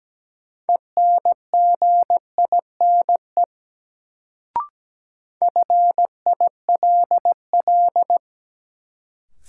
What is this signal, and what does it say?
Morse code, heard as a keyed tone, about 17 wpm, 700 Hz. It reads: ENGINE FILL
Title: Sentences from Top 1500 Words - New Word Drill Alt - 17wpm